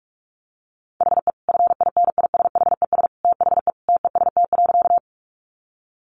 Morse code, heard as a keyed tone, about 44 wpm, 700 Hz. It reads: HE FINISHES THE TEST.
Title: Additional Easy Phrases 45wpm